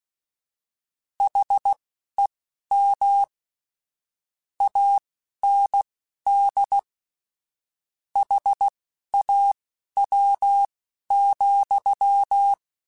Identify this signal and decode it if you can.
Morse code, heard as a keyed tone, about 16 wpm, spaced as for 11 wpm, 800 Hz. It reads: HEM AND HAW,
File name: Misc / Misc-mp3s / Word Pairs @10